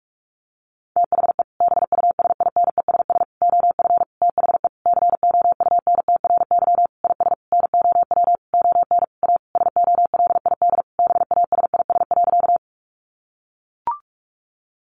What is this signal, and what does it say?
Morse code, heard as a keyed tone, about 44 wpm, 700 Hz. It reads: THE BUSINESS OF THE COUNTRY IS NOW ON A SOLID BASIS.
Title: Sentences from Top 1500 Words 45wpm